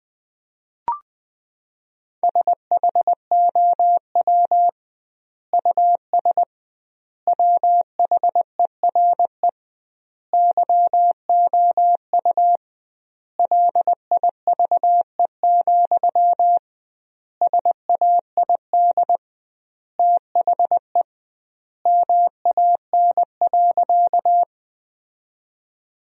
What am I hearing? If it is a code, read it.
Morse code, heard as a keyed tone, about 21 wpm, 700 Hz. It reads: SHOW US WHERE YOU LIVE, SAID THE MAN.
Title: 2x Word-Spacing - Sentences from Top 200 Words - 20wpm